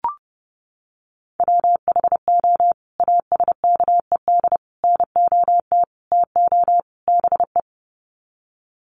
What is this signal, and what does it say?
Morse code, heard as a keyed tone, about 30 wpm, 700 Hz. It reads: WHO ASKED NOT TO BE
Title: Sets of 5 Words Encore2 30wpm